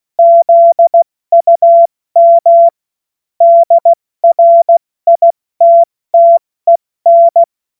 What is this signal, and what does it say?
Morse code, heard as a keyed tone, about 16 wpm, 680 Hz. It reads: ZUM DRITTEN
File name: Morsekurs / Witze / Witz 21. März 2023 70 F80